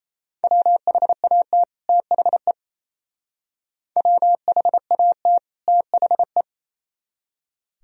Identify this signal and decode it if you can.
Morse code, heard as a keyed tone, about 36 wpm, 690 Hz. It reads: WHAT THE WHAT THE